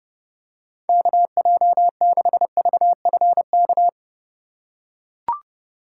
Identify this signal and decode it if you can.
Morse code, heard as a keyed tone, about 30 wpm, 700 Hz. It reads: KJ6VFK
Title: USA Call Signs - 6 Characters - 4x Character Spacing - 30wpm